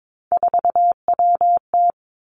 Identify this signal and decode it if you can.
Morse code, heard as a keyed tone, about 22 wpm, 700 Hz. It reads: 4WT
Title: Three Challenging Letter-Number Combinations 22wpm